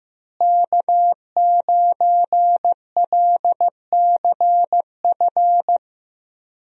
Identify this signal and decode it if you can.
Morse code, heard as a keyed tone, about 15 wpm, 700 Hz. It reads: K9LCF
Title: USA Call Signs - 5 Characters - Repeated 2x - 15wpm